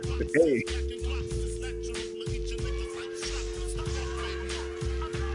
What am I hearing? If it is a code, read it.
Dial tone (North America), 350 + 440 Hz.